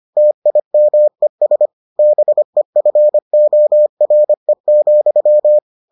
Morse code, heard as a keyed tone, about 25 wpm, 600 Hz. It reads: TIMES BEFORE,